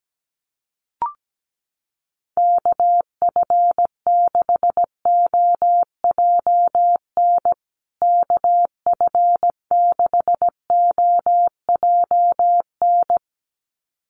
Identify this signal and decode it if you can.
Morse code, heard as a keyed tone, about 17 wpm, 700 Hz. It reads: KF6OJN KF6OJN
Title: USA Call Signs Repeated 2x 17wpm